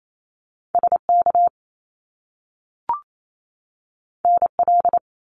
Morse code, heard as a keyed tone, about 28 wpm, 700 Hz. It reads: SK NL
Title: Canadian Provinces and Territories 28wpm